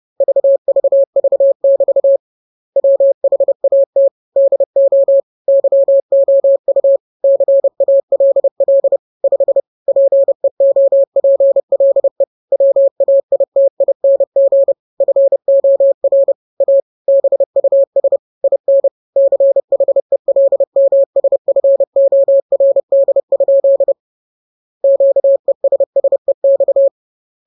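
Morse code, heard as a keyed tone, about 30 wpm, 550 Hz. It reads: VVV<BT> WHAT DO YOU CALL 5 PEOPLE WAITING FOR A BUS IN CHELMSFORD? QESSEX
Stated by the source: Morse code joke sent at 30 wpm.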